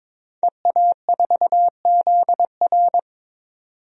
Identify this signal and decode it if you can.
Morse code, heard as a keyed tone, about 22 wpm, 700 Hz. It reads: EA4ZR
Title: DXCC-100 Call Signs - 5 Characters - 22wpm